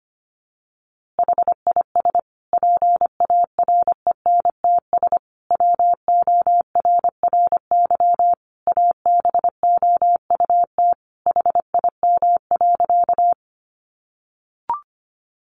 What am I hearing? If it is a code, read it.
Morse code, heard as a keyed tone, about 25 wpm, 700 Hz. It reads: HIS PARENTS WORRY ABOUT HIM.
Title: Easy Phrases 25wpm